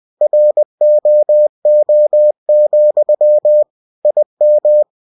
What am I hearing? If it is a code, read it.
Morse code, heard as a keyed tone, about 20 wpm, 600 Hz. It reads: ROO, IM
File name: winnie-20wpm